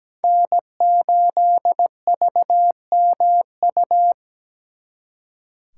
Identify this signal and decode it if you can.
Morse code, heard as a keyed tone, about 20 wpm, 700 Hz. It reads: N8VMU
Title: Speed Racing - USA 1x3 Call Signs 17wpm